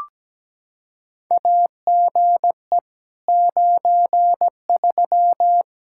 Morse code, heard as a keyed tone, about 17 wpm, 700 Hz. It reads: AGE 93
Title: K1USN SST QSO Practice 17wpm